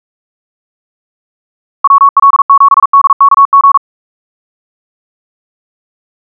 Morse code, heard as a keyed tone, about 42 wpm, 1110 Hz. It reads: WL7DKG
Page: SampleHST1.wav <-- 158cpm (=47wpm(paris)) 1111Hz sinusoidal waveform